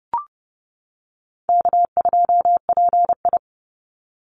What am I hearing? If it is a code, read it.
Morse code, heard as a keyed tone, about 30 wpm, 700 Hz. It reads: K2PI
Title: Easy Call Signs - 30wpm